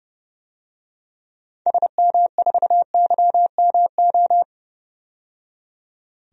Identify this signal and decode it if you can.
Morse code, heard as a keyed tone, about 30 wpm, 700 Hz. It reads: SM4YMO
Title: DXCC-100 Call Signs - 6 Characters - 30wpm